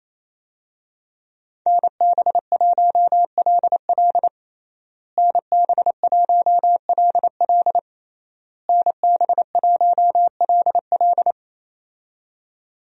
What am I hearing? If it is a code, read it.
Morse code, heard as a keyed tone, about 28 wpm, 700 Hz. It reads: NB1LL NB1LL NB1LL
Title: USA Call Signs - 5 Characters - Repeated 3x - 28wpm